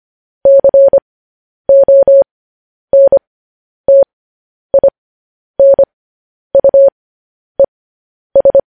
Morse code, heard as a keyed tone, about 25 wpm, spaced as for 10 wpm, 550 Hz. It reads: CONTINUES
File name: podcast / News headlines-1x-25x10